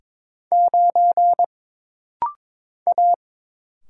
Morse code, heard as a keyed tone, about 22 wpm, 700 Hz. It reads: 9 A
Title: Single Letter-Number - Rapid-Fire 22wpm